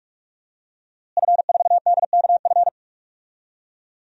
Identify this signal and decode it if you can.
Morse code, heard as a keyed tone, about 47 wpm, 700 Hz. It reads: F4DKF
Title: DXCC-100 Call Signs - 5 Characters - Repeated 2x - 45wpm